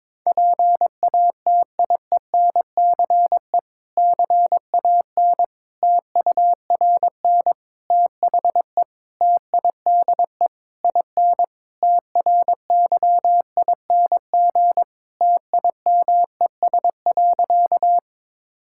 Morse code, heard as a keyed tone, about 21 wpm, 700 Hz. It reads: PATIENCE CAN TURN THE TIDE IN TRYING TIMES.
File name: Difficult Sentences with Hints - 22wpm